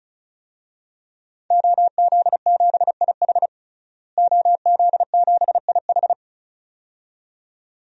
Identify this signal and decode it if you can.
Morse code, heard as a keyed tone, about 36 wpm, 700 Hz. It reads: OZ7IH OZ7IH
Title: DXCC-100 Call Signs - 5 Characters - Repeated 2x - 35wpm